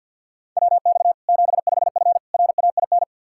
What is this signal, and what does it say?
Morse code, heard as a keyed tone, about 52 wpm, 700 Hz. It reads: WX 75F RAIN
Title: Practice QSO - Exchange 2 - Info - 50wpm